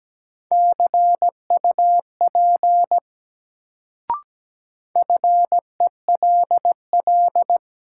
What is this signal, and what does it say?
Morse code, heard as a keyed tone, about 17 wpm, 700 Hz. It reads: CUP FELL
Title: Sentences from Top 1600 Words - New Word Drill Alt - 17wpm